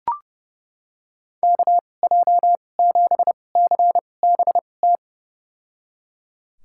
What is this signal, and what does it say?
Morse code, heard as a keyed tone, about 30 wpm, spaced as for 22 wpm, 700 Hz. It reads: KJ7CBT
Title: USA Call Signs - 6 Characters - 2x Character Spacing - 30wpm